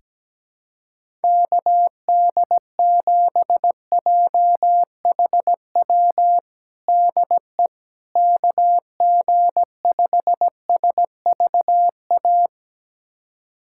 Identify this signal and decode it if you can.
Morse code, heard as a keyed tone, about 17 wpm, 700 Hz. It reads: KD7JHW DE KG5SVA
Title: USA Callsign DE Callsign - 6 Characters - 17wpm